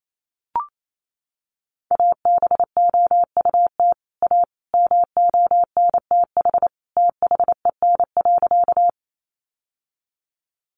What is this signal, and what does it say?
Morse code, heard as a keyed tone, about 28 wpm, 700 Hz. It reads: ABOUT A MONTH THEN.
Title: Sentences from Top 800 Words 28wpm